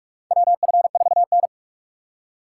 Morse code, heard as a keyed tone, about 46 wpm, 700 Hz. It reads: WF4N
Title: USA Call Signs - 4 Characters - 6x Character Spacing - 45wpm